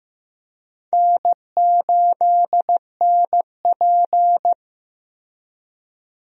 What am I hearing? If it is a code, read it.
Morse code, heard as a keyed tone, about 15 wpm, 700 Hz. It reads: N8NP
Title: USA Call Signs - 4 Characters - Repeated 3x - 15wpm